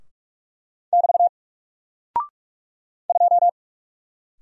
Morse code, heard as a keyed tone, about 43 wpm, 700 Hz. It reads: <BT> 2
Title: Single Character (Letter:Number:Punctuation:Prosign) Alt 45wpm